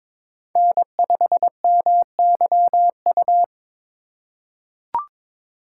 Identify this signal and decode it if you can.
Morse code, heard as a keyed tone, about 22 wpm, 700 Hz. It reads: N5MYU
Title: USA Call Signs - 5 Characters - 4x Character Spacing - 22wpm